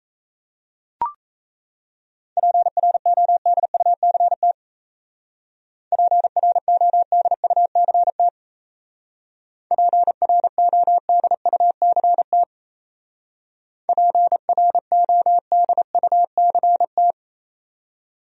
Morse code, heard as a keyed tone, about 37 wpm, 700 Hz. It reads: PRODUCT PRODUCT PRODUCT PRODUCT